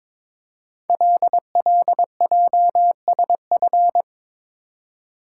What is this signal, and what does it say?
Morse code, heard as a keyed tone, about 22 wpm, 700 Hz. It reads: LLJSF
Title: Code Groups - 5 Letters - 22wpm